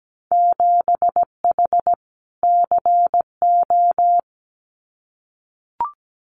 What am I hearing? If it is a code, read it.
Morse code, heard as a keyed tone, about 17 wpm, 700 Hz. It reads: 7H CO
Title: Winter Field Day Exchanges Encore 17wpm